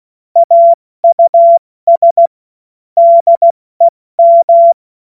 Morse code, heard as a keyed tone, about 16 wpm, spaced as for 14 wpm, 680 Hz. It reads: AUS DEM
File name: Morsekurs / Witze / Witz 28. Dezember 2021, 70 F80